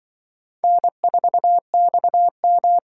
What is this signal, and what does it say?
Morse code, heard as a keyed tone, about 24 wpm, 700 Hz. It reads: N4XM
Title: Speed Racing - USA 1x2 Call Signs 20wpm